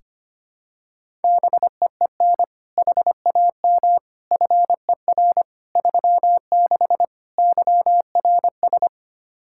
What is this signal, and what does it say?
Morse code, heard as a keyed tone, about 25 wpm, 700 Hz. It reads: BEEN HAM FER 36 YRS
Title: Practice QSO - Exchange 3 - More Info - 25wpm